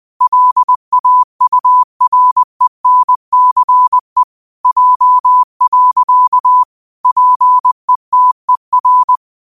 Morse code, heard as a keyed tone, about 20 wpm, 1000 Hz. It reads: LAURENCE J. PETER
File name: qotdfiles / 20wpm